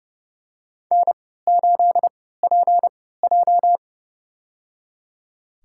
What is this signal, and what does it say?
Morse code, heard as a keyed tone, about 30 wpm, spaced as for 17 wpm, 700 Hz. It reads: N8PJ
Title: USA Call Signs - 4 Characters - 3x Character Spacing - 30wpm